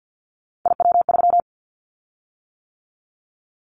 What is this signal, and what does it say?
Morse code, heard as a keyed tone, about 50 wpm, 700 Hz. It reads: IW3
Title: Callsign Prefixes for Top 50 DXCC - 50wpm